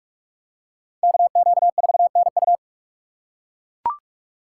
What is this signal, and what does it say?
Morse code, heard as a keyed tone, about 40 wpm, 700 Hz. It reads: KQ4NU